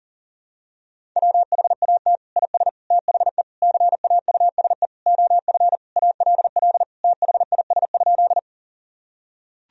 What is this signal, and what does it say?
Morse code, heard as a keyed tone, about 40 wpm, 700 Hz. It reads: WHAT IS THE CAUSE OF ALL THIS?
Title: Sentences from Top 1000 Words - Review Part 1 - 40wpm